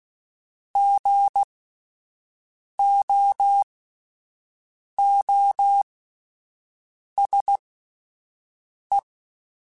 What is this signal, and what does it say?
Morse code, heard as a keyed tone, about 16 wpm, spaced as for 5 wpm, 800 Hz. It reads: GOOSE
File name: Alphabetical / Alphabetical mp3s / Gs @5